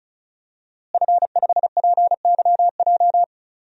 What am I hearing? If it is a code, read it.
Morse code, heard as a keyed tone, about 36 wpm, 700 Hz. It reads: F5PYJ
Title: DXCC-100 Call Signs - 5 Characters - Repeated 2x - 35wpm